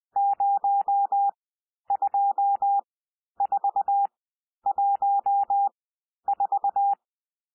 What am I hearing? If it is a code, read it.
Morse code, heard as a keyed tone, about 20 wpm, spaced as for 11 wpm, 800 Hz. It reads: 02414
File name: cwmp3 / N1 SOUND NUMBERS 20 wpm